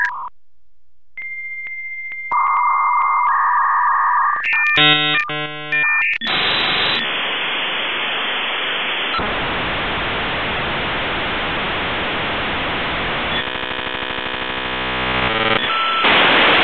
Dial-up modem handshake.